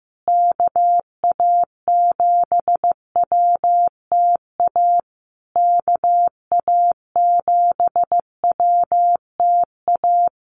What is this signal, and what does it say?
Morse code, heard as a keyed tone, about 15 wpm, 700 Hz. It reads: KA7WTA KA7WTA
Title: USA Call Signs Repeated 3x 15wpm